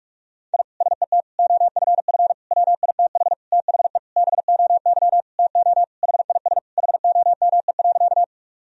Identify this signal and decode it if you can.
Morse code, heard as a keyed tone, about 46 wpm, 700 Hz. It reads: I SET OFF WITH THE BOY TO HIS HOME.
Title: Sentences from Top 1000 Words - Review Part 1 - 45wpm